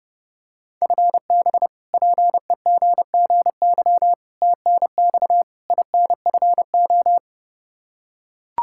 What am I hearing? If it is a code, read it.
Morse code, heard as a keyed tone, about 32 wpm, 700 Hz. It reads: FB PEGGY TNX INFO
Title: Practice QSO - Exchange 3 - More Info - 30wpm